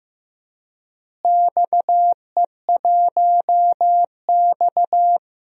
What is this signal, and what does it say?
Morse code, heard as a keyed tone, about 15 wpm, 700 Hz. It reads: XE1X
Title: DXCC-100 Call Signs - 4 Characters - Repeated 3x - 15wpm